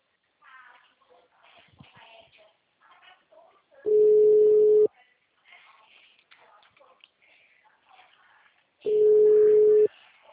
Ringback tone (Europe, 425 Hz).